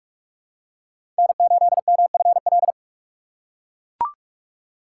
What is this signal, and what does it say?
Morse code, heard as a keyed tone, about 42 wpm, 700 Hz. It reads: N8MFL
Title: USA Call Signs - 5 Characters - 4x Character Spacing - 45wpm